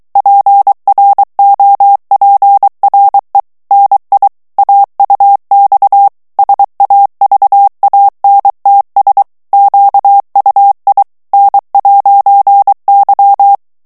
Morse code, heard as a keyed tone, about 23 wpm, 800 Hz. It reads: PROPRE NI AUX SAVANTS QUI N'Y
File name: CW / FICHIERS-MP3 / FRANCAIS TEXTE CLAIR 2 VITESSE 22 MOTS MINUTES